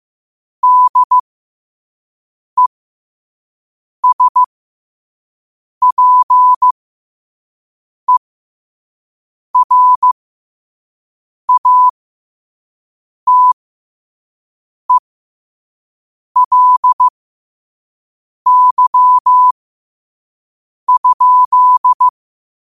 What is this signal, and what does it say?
Morse code, heard as a keyed tone, about 15 wpm, spaced as for 5 wpm, 1000 Hz. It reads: DESPERATELY?
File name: qotdfiles / 5wpm